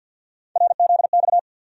Morse code, heard as a keyed tone, about 46 wpm, 700 Hz. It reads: RZX
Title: USA Call Sign Postfixes - 3 Letters 50wpm